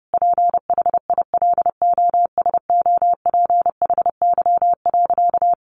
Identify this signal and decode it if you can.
Morse code, heard as a keyed tone, about 30 wpm, 700 Hz. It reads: PHILOSOPHY.
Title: Sentences from Top 900 Words 30wpm